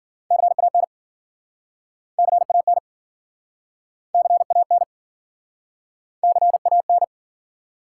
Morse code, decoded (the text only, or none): CAN CAN CAN CAN